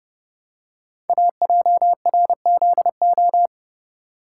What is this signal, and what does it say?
Morse code, heard as a keyed tone, about 28 wpm, 700 Hz. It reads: AJRZO
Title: Code Groups - 5 Letter:Numbers - 30wpm